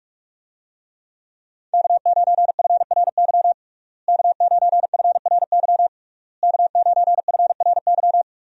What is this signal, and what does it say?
Morse code, heard as a keyed tone, about 46 wpm, 700 Hz. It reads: K9FRY K9FRY K9FRY
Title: USA Call Signs - 5 Characters - Repeated 3x - 45wpm